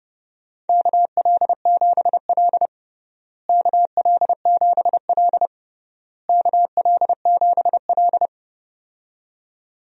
Morse code, heard as a keyed tone, about 32 wpm, 700 Hz. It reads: KL7L KL7L KL7L
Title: DXCC-100 Call Signs - 4 Characters - Repeated 3x - 30wpm